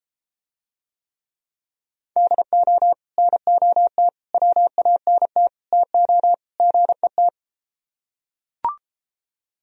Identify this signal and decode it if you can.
Morse code, heard as a keyed tone, about 33 wpm, 700 Hz. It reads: DO NOT WANT TO GET